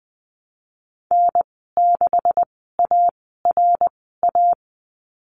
Morse code, heard as a keyed tone, about 20 wpm, spaced as for 15 wpm, 700 Hz. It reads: N6ARA
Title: 167 Lesson 27g - Number 7 - Call Signs 20wpm